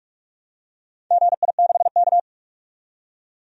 Morse code, heard as a keyed tone, about 43 wpm, 700 Hz. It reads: GI6K
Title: DXCC-100 Call Signs - 4 Characters - 45wpm